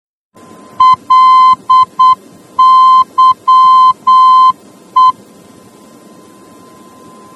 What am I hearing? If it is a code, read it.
Morse code, heard as a keyed tone, about 8 wpm, 1030 Hz. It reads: LYE